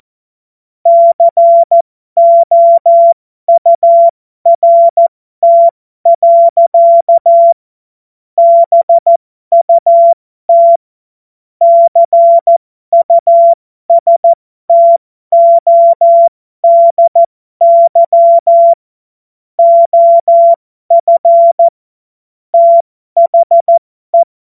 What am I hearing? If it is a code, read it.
Morse code, heard as a keyed tone, about 14 wpm, spaced as for 12 wpm, 680 Hz. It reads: COURT. BUT CUSTODY OF THE
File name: Morsekurs / Witze